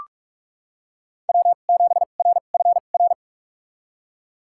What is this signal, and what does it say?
Morse code, heard as a keyed tone, about 41 wpm, spaced as for 31 wpm, 700 Hz. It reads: W7RFR
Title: USA Call Signs - 5 Characters - 2x Character Spacing - 45wpm